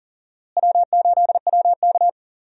Morse code, heard as a keyed tone, about 40 wpm, 700 Hz. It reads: W8WK
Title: USA Call Signs - 4 Characters - Repeated 2x - 40wpm